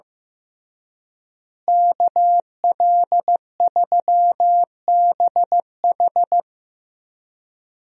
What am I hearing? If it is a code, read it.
Morse code, heard as a keyed tone, about 15 wpm, 700 Hz. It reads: KL3BH
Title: USA Call Signs - 5 Characters - Repeated 3x - 15wpm